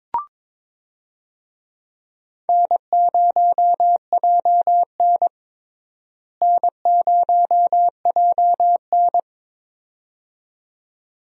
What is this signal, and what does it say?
Morse code, heard as a keyed tone, about 22 wpm, 700 Hz. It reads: N0JN N0JN
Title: USA Call Signs - 4 Characters - Repeated 2x - 22wpm